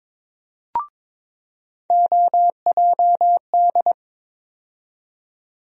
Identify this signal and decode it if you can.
Morse code, heard as a keyed tone, about 23 wpm, 700 Hz. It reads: OJD
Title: USA Call Sign Postfixes - 3 Letters 22wpm